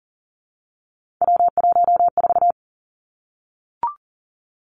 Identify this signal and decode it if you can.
Morse code, heard as a keyed tone, about 40 wpm, 700 Hz. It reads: W14
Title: Three Challenging Letter-Number Combinations 40wpm